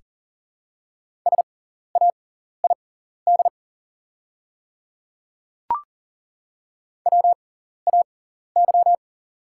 Morse code, heard as a keyed tone, about 40 wpm, spaced as for 14 wpm, 700 Hz. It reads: SAID WAY
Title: 6x Character Spacing - Top 100 Words - 40wpm